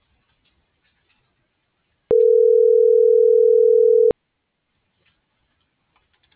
Ringback tone (North America), 440 + 480 Hz.